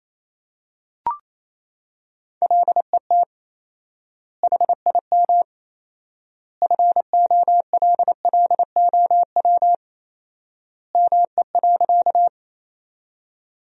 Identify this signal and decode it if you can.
Morse code, heard as a keyed tone, about 27 wpm, 700 Hz. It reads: LET HIM FOLLOW ME.
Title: 4x Word-Spacing - Sentences from Top 300 Words - 28wpm